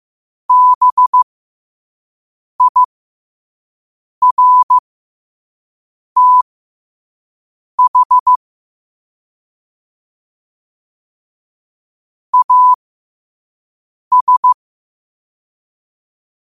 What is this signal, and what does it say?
Morse code, heard as a keyed tone, about 15 wpm, spaced as for 5 wpm, 1000 Hz. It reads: BIRTH AS
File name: qotdfiles / 5wpm